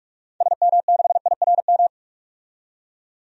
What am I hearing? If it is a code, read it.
Morse code, heard as a keyed tone, about 46 wpm, 700 Hz. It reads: SM6IRM
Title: DXCC-100 Call Signs - 6 Characters - 45wpm